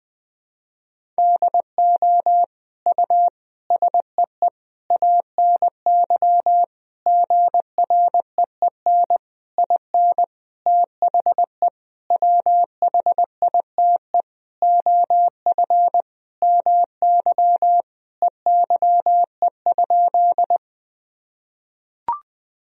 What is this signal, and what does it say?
Morse code, heard as a keyed tone, about 21 wpm, 700 Hz. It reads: DO U SEE ANY GREEN IN THE WHITE OF MY EYE?
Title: Sentences from Top 500 Words and Abbreviations 20wpm